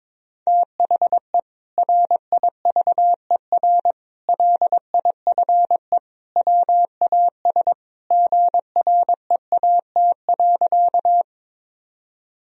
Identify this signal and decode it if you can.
Morse code, heard as a keyed tone, about 22 wpm, 700 Hz. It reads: THE RIVER LIFE WAS GREAT.
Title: Sentences from Top 1000 Words - Review Part 1 - 22wpm